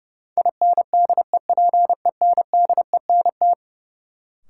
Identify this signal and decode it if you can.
Morse code, heard as a keyed tone, about 30 wpm, 700 Hz. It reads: INDEPENDENT
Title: Sentences from Top 900 Words - New Word Drill - 30wpm